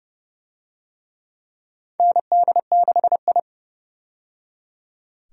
Morse code, heard as a keyed tone, about 30 wpm, 700 Hz. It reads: ND6I
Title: USA Call Signs - 4 Characters - 30wpm